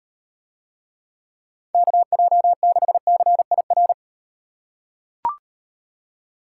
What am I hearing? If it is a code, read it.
Morse code, heard as a keyed tone, about 38 wpm, 700 Hz. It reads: KJ6CIR